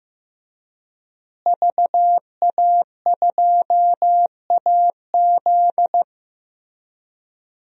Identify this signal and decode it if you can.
Morse code, heard as a keyed tone, about 16 wpm, 700 Hz. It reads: VA2AZ
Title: DXCC-100 Call Signs - 5 Characters - Repeated 3x - 15wpm